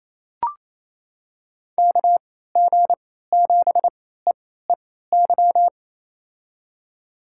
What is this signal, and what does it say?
Morse code, heard as a keyed tone, about 28 wpm, spaced as for 16 wpm, 700 Hz. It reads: KG7EEY
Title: USA Call Signs - 6 Characters - 3x Character Spacing - 28wpm